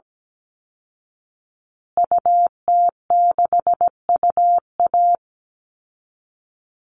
Morse code, heard as a keyed tone, about 17 wpm, 700 Hz. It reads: UT6UA
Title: Call Signs Repeated 3x - 17wpm